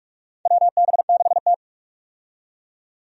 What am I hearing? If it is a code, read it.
Morse code, heard as a keyed tone, about 46 wpm, 700 Hz. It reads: WB6T
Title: USA Call Signs - 4 Characters - 45wpm